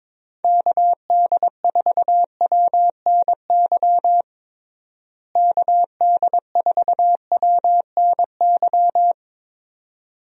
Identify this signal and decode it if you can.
Morse code, heard as a keyed tone, about 22 wpm, 700 Hz. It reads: KD4WNY KD4WNY
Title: USA Call Signs - 6 Characters - Repeated 2x - 22wpm